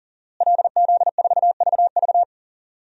Morse code, heard as a keyed tone, about 37 wpm, 700 Hz. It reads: LZ4VV
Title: DXCC-100 Call Signs - 5 Characters - Repeated 2x - 40wpm